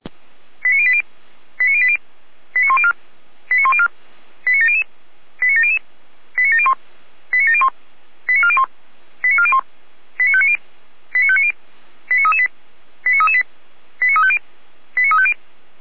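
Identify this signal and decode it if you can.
Five-tone selective call (ZVEI), heard as quick runs of five tones, tones of about 70 ms. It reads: 89080 89080 89104 89104 89700 89700 89701 89701 89401 89401 89590 89590 89338 89338 89350 89350
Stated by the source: Received on an SDR, ZVEI five-tone selective call, code 89120.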